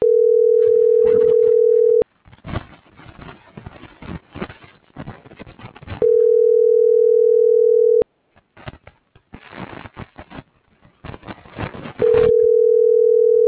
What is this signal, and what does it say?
Ringback tone (North America), 440 + 480 Hz.
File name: CENTRAL MULTIMODAL / out josimar.cruz - .76481